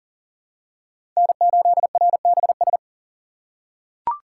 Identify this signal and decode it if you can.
Morse code, heard as a keyed tone, about 40 wpm, 700 Hz. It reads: N8RBS
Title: USA Call Signs - 5 Characters - 4x Character Spacing - 40wpm